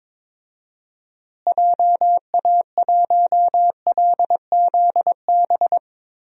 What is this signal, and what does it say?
Morse code, heard as a keyed tone, about 22 wpm, 700 Hz. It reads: JA1LZB
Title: DXCC-100 Call Signs - 6 Characters - Repeated 2x - 22wpm